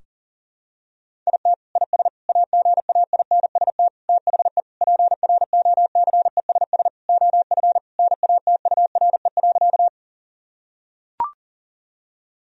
Morse code, heard as a keyed tone, about 40 wpm, 700 Hz. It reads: IT IS AGAINST THE PROCESS OF NATURE.
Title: Sentences from Top 1000 Words - Review Part 3 - 40wpm